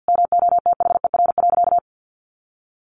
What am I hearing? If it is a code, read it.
Morse code, heard as a keyed tone, about 48 wpm, 700 Hz. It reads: MOTHER.
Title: Sentences from Top 600 Words 50wpm